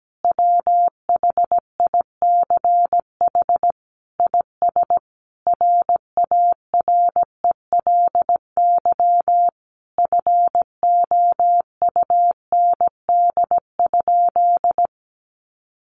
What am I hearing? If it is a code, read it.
Morse code, heard as a keyed tone, about 17 wpm, 700 Hz. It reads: WHICH IS RARELY FOUND?
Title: Sentences from Top 1400 Words 17wpm